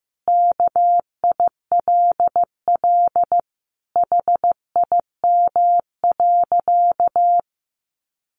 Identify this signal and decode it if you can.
Morse code, heard as a keyed tone, about 15 wpm, 700 Hz. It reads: KILL HIM.
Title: Sentences from Top 1400 Words 15wpm